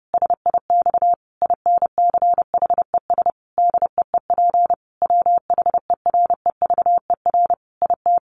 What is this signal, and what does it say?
Morse code, heard as a keyed tone, about 30 wpm, 700 Hz. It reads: SIX INCHES DEEP WHEREVER IT
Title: A Sound Of Thunder 30wpm